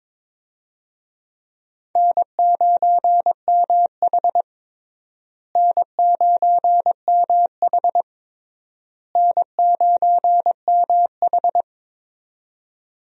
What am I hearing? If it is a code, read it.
Morse code, heard as a keyed tone, about 21 wpm, 700 Hz. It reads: N9MH N9MH N9MH
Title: USA Call Signs - 4 Characters - Repeated 3x - 22wpm